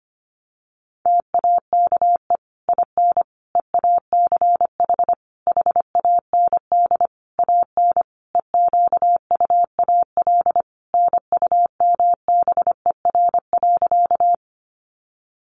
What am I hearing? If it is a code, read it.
Morse code, heard as a keyed tone, about 25 wpm, 700 Hz. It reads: TAKE IN EACH HAND AN EQUAL NUMBER.
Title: Sentences from Top 1400 Words 25wpm